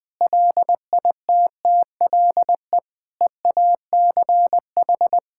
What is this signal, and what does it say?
Morse code, heard as a keyed tone, about 20 wpm, 700 Hz. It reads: LITTLE EACH
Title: Difficult Sentences with Hints - 20wpm